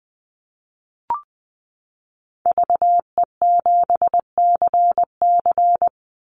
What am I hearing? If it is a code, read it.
Morse code, heard as a keyed tone, about 20 wpm, 700 Hz. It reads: VE7CC
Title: Call Signs - 20wpm